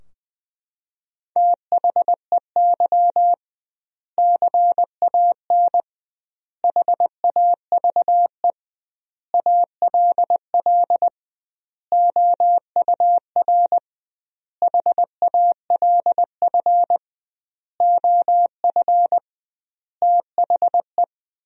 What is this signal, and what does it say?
Morse code, heard as a keyed tone, about 20 wpm, 700 Hz. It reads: THEY CAN HAVE ALL OUR HALF OF THE
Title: 2x Word-Spacing - Sentences from Top 400 Words - 20wpm